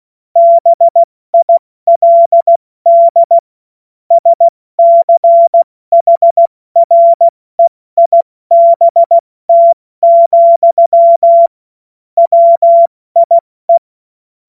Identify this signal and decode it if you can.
Morse code, heard as a keyed tone, about 16 wpm, 680 Hz. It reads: BILD SCHREIBT, WIE